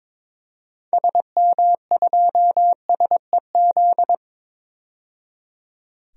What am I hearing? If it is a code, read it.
Morse code, heard as a keyed tone, about 21 wpm, 700 Hz. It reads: SM2SEZ
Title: DXCC-100 Call Signs - 6 Characters - 22wpm